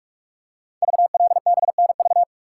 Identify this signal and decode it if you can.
Morse code, heard as a keyed tone, about 44 wpm, 690 Hz. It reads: VLBNV
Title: Code Groups - 5 Letter:Numbers - 45wpm